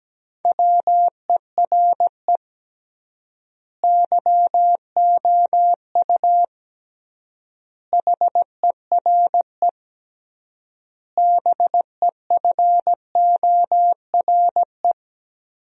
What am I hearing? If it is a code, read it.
Morse code, heard as a keyed tone, about 17 wpm, 700 Hz. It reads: WERE YOU HERE BEFORE
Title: 3x Word-Spacing - Sentences from Top 200 Words - 17wpm